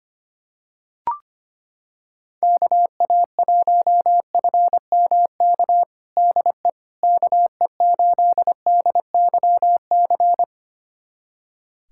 Morse code, heard as a keyed tone, about 25 wpm, 700 Hz. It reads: KA1FMK DE KE8DYC
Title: USA Callsign DE Callsign - 6 Characters - 25wpm